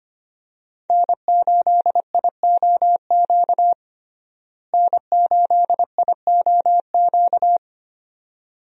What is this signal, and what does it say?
Morse code, heard as a keyed tone, about 25 wpm, 700 Hz. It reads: N8IOQ N8IOQ
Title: USA Call Signs - 5 Characters - Repeated 2x - 25wpm